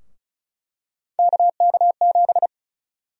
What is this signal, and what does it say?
Morse code, heard as a keyed tone, about 34 wpm, 700 Hz. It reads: KK7
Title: USA Call Sign Prefixes - Repeated 2x - 35wpm